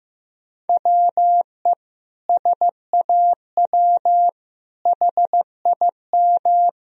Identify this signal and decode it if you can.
Morse code, heard as a keyed tone, about 15 wpm, 700 Hz. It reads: WE SAW HIM
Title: Sentences from Top 1000 Words - Review Part 4 - 15wpm